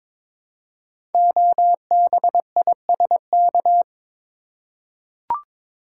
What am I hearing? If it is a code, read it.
Morse code, heard as a keyed tone, about 22 wpm, 700 Hz. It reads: OBISK
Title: Code Groups - 5 Letters - 22wpm